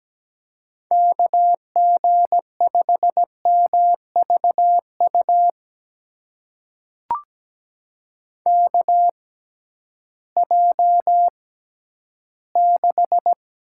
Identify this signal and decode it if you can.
Morse code, heard as a keyed tone, about 17 wpm, spaced as for 11 wpm, 700 Hz. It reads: KG5MVU KJ6
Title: USA Call Signs - 6x Character Spacing - 17wpm